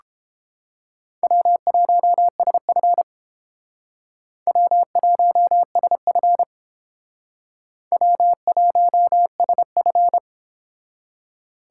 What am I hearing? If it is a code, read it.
Morse code, heard as a keyed tone, about 30 wpm, 700 Hz. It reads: W1SF W1SF W1SF